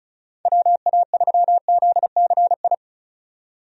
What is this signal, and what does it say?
Morse code, heard as a keyed tone, about 34 wpm, 700 Hz. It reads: WA3ZCI
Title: USA Call Signs - 6 Characters - 35wpm